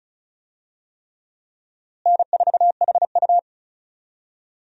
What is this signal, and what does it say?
Morse code, heard as a keyed tone, about 34 wpm, 700 Hz. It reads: N4HU
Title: USA Call Signs - 4 Characters - Repeated 3x - 35wpm